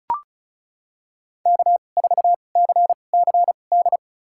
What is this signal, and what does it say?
Morse code, heard as a keyed tone, about 34 wpm, spaced as for 25 wpm, 700 Hz. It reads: K4CCD
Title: USA Call Signs - 5 Characters - 2x Character Spacing - 35wpm